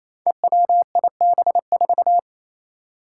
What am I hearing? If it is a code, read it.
Morse code, heard as a keyed tone, about 28 wpm, 700 Hz. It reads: EWIB4
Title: Five Challenging Letter-Number Combinations 28wpm